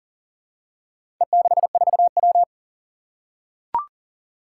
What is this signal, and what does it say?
Morse code, heard as a keyed tone, about 42 wpm, 700 Hz. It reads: E64W
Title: Four Challenging Letter-Number Combinations 40wpm